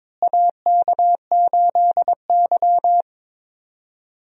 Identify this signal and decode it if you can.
Morse code, heard as a keyed tone, about 22 wpm, 700 Hz. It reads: AK8Y
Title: USA Call Signs - 4 Characters - Repeated 2x - 22wpm